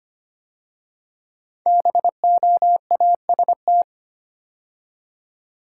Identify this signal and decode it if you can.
Morse code, heard as a keyed tone, about 25 wpm, 700 Hz. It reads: BOAST
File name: 5 Letter Words - First 3 Letters Given - 25wpm